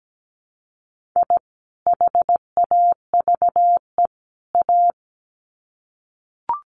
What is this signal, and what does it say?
Morse code, heard as a keyed tone, about 17 wpm, 700 Hz. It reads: I HAVE A
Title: Sets of 3 Words 17wpm